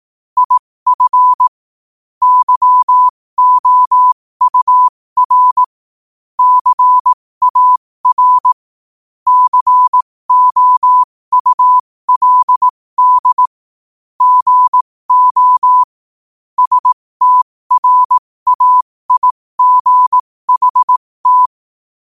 Morse code, heard as a keyed tone, about 18 wpm, spaced as for 15 wpm, 1000 Hz. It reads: IF YOUR CAR COULD GO STRAIGHT
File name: qotdfiles / 15wpm